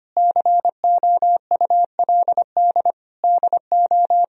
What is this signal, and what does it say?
Morse code, heard as a keyed tone, about 25 wpm, 700 Hz. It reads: COULD DO
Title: Sentences from Top 1000 Words - Review Part 4 - 25wpm